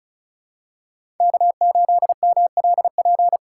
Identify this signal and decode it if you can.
Morse code, heard as a keyed tone, about 34 wpm, 700 Hz. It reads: K8MLP
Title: USA Call Signs - 5 Characters - 3x Character Spacing - 35wpm